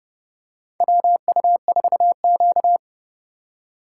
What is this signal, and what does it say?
Morse code, heard as a keyed tone, about 30 wpm, 700 Hz. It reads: WU4Q
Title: USA Call Signs - 4 Characters - 4x Character Spacing - 30wpm